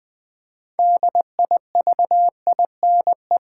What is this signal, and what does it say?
Morse code, heard as a keyed tone, about 20 wpm, 700 Hz. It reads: DIVINE
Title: Build A Word - 3 Letters At A Time - 6-Letter Words - 20wpm